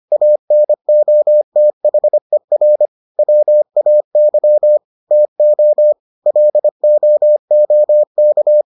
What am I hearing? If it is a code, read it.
Morse code, heard as a keyed tone, about 25 wpm, 600 Hz. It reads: ANOTHER WAY TO LOOK